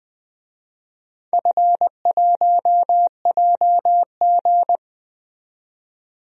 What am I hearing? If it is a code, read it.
Morse code, heard as a keyed tone, about 20 wpm, 700 Hz. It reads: F1JG
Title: DXCC-100 Call Signs - 4 Characters - 20wpm